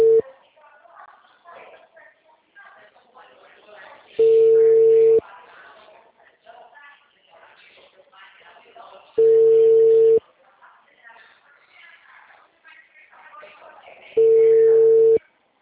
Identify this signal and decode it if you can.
Ringback tone (China, 450 Hz), with speech.
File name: out- -2012- - - .31051